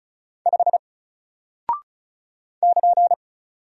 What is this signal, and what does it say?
Morse code, heard as a keyed tone, about 36 wpm, 700 Hz. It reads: 5 <KN>
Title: Single Character (Letter:Number:Punctuation:Prosign) Alt 35wpm